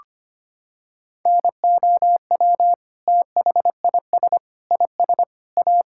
Morse code, heard as a keyed tone, about 25 wpm, 700 Hz. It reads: NOW THIS IS A
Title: Sentences from Top 1000 Words - Review Part 6 - 25wpm